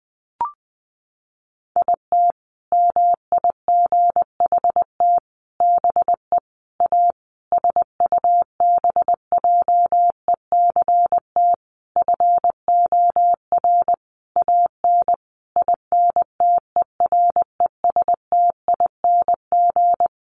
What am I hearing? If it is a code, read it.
Morse code, heard as a keyed tone, about 20 wpm, 700 Hz. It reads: IT MIGHT BE A SUBJECT FOR AN INTERESTING
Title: Sentences from Top 1500 Words 20wpm